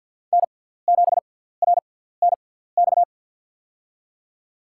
Morse code, heard as a keyed tone, about 47 wpm, spaced as for 17 wpm, 700 Hz. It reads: N7RNX
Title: USA Call Signs - 6x Character Spacing - 50wpm